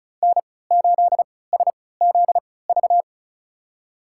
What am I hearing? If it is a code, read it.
Morse code, heard as a keyed tone, about 34 wpm, spaced as for 20 wpm, 700 Hz. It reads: N8SZV
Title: USA Call Signs - 5 Characters - 3x Character Spacing - 35wpm